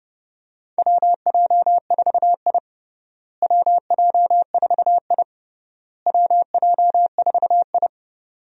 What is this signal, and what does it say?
Morse code, heard as a keyed tone, about 30 wpm, 700 Hz. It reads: WJ4I WJ4I WJ4I
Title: USA Call Signs - 4 Characters - Repeated 3x - 30wpm